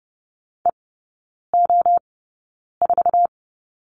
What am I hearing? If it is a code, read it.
Morse code, heard as a keyed tone, about 30 wpm, spaced as for 9 wpm, 700 Hz. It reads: EO4
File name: 081 Lesson 14e - Number 2 - Three Characters 30wpm